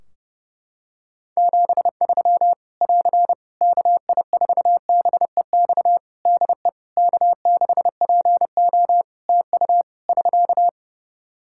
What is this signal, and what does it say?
Morse code, heard as a keyed tone, about 30 wpm, 700 Hz. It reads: 73 <AR> KI4BEX DE K6PO TU <SK>
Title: Practice QSO - Exchange 4 - Ending - 30wpm